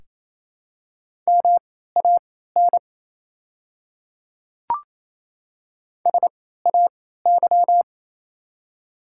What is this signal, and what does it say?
Morse code, heard as a keyed tone, about 28 wpm, spaced as for 16 wpm, 700 Hz. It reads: MAN SAY
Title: 3x Character Spacing - Top 200 Words - 28wpm